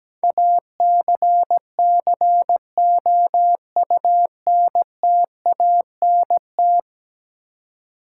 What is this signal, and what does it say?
Morse code, heard as a keyed tone, about 17 wpm, 700 Hz. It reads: ACCOUNTANT
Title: Practice QSO - Exchange 3 - More Info - 17wpm